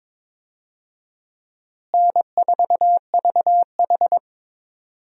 Morse code, heard as a keyed tone, about 22 wpm, 700 Hz. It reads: N4VH
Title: USA Call Signs - 4 Characters - Repeated 3x - 22wpm